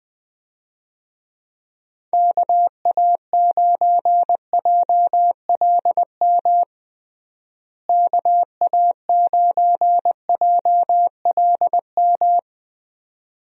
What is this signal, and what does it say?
Morse code, heard as a keyed tone, about 21 wpm, 700 Hz. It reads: KA9JLM KA9JLM
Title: USA Call Signs - 6 Characters - Repeated 2x - 20wpm